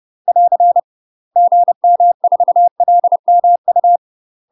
Morse code, heard as a keyed tone, about 30 wpm, 700 Hz. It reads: <AR> GM4LMU
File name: UK MASTER QSOs TRAINING / 30WPM / UK MASTER QSOs 0013